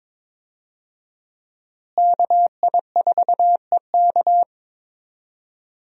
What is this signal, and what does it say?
Morse code, heard as a keyed tone, about 22 wpm, 700 Hz. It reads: KI4EK
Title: USA Call Signs - 5 Characters - Repeated 3x - 22wpm